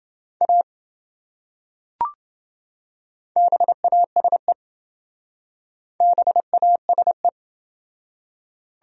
Morse code, heard as a keyed tone, about 28 wpm, 700 Hz. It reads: A BASE BASE